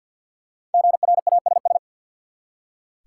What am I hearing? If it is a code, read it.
Morse code, heard as a keyed tone, about 47 wpm, 700 Hz. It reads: GRASS
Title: Five Letter Words - 50wpm